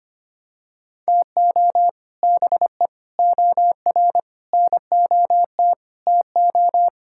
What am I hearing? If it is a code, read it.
Morse code, heard as a keyed tone, about 25 wpm, 700 Hz. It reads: TO BE OR NOT TO
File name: practice / 25wpm / Sentences with 2-Word Phrases - 25wpm